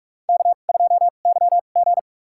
Morse code, heard as a keyed tone, about 46 wpm, spaced as for 33 wpm, 700 Hz. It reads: K2YG
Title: USA Call Signs - 4 Characters - 2x Character Spacing - 45wpm